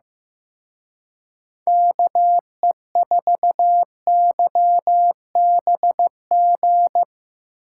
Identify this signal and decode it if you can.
Morse code, heard as a keyed tone, about 16 wpm, 700 Hz. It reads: KE4YBG
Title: Speed Racing - USA 2x3 Call Signs 15wpm